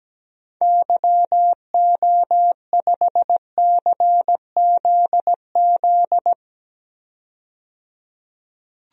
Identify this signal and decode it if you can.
Morse code, heard as a keyed tone, about 17 wpm, 700 Hz. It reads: YO5CZZ
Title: DXCC-100 Call Signs - 6 Characters - 17wpm